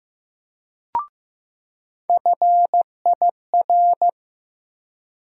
Morse code, heard as a keyed tone, about 15 wpm, 700 Hz. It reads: FIR
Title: USA Call Sign Postfixes - 3 Letters 15wpm